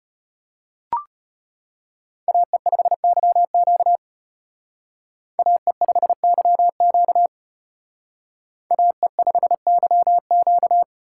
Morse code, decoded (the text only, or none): AE5YQ AE5YQ AE5YQ